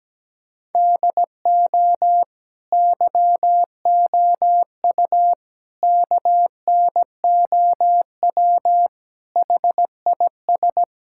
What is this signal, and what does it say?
Morse code, heard as a keyed tone, about 17 wpm, 700 Hz. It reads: DO YOU KNOW HIS
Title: Sentences from Top 1000 Words - Review Part 6 - 17wpm